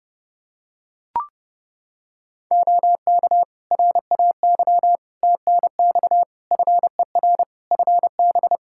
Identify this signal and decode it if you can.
Morse code, heard as a keyed tone, about 32 wpm, 700 Hz. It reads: OK RAY TNX FER FB
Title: Practice QSO - Exchange 4 - Ending - 30wpm